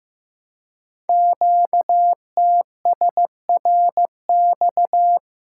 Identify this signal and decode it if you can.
Morse code, heard as a keyed tone, about 15 wpm, 700 Hz. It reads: QTSRX
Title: Code Groups - 5 Letters - 15wpm